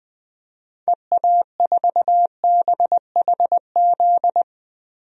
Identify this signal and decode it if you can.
Morse code, heard as a keyed tone, about 21 wpm, 700 Hz. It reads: EA4BHZ
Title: DXCC-100 Call Signs - 6 Characters - 20wpm